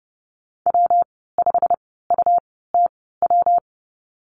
Morse code, heard as a keyed tone, about 32 wpm, spaced as for 17 wpm, 700 Hz. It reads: W5UTW
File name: Lesson 19g - Letter w - Call Signs 30wpm